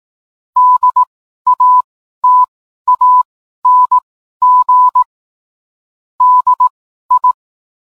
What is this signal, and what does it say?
Morse code, heard as a keyed tone, about 18 wpm, spaced as for 12 wpm, 1000 Hz. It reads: DATANG DI